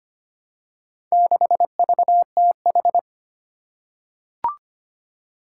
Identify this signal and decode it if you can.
Morse code, heard as a keyed tone, about 25 wpm, 700 Hz. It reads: 6VTH
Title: Four Challenging Letter-Number Combinations 25wpm